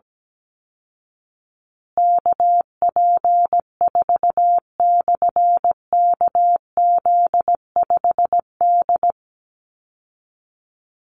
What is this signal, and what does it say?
Morse code, heard as a keyed tone, about 17 wpm, 700 Hz. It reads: KP4/KZ5D
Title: Call Signs Repeated 3x - 17wpm